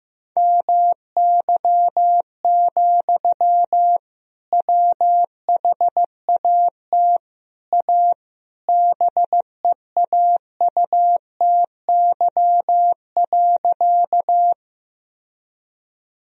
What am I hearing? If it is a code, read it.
Morse code, heard as a keyed tone, about 15 wpm, 700 Hz. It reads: MY, WHAT A BEAUTY.
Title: Sentences from Top 1000 Words - Review Part 2 - 15wpm